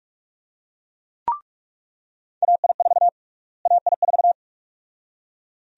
Morse code, heard as a keyed tone, about 45 wpm, 700 Hz. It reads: AI4 AI4
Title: USA Call Sign Prefixes - Repeated 2x - 45wpm